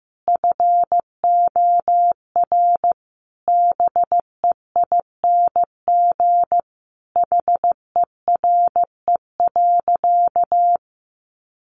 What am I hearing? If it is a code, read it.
Morse code, heard as a keyed tone, about 15 wpm, 700 Hz. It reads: FOR BEING HERE.
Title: Sentences from Top 700 Words 15wpm